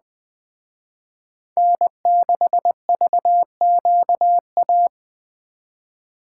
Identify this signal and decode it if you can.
Morse code, heard as a keyed tone, about 20 wpm, 700 Hz. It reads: N6VQA
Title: Speed Racing - USA 1x3 Call Signs 17wpm